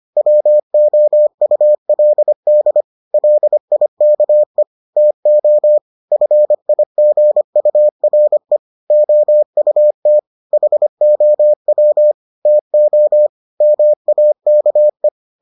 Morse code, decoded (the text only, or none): WOULD LIKE TO FIGURE OUT HOW TO MAKE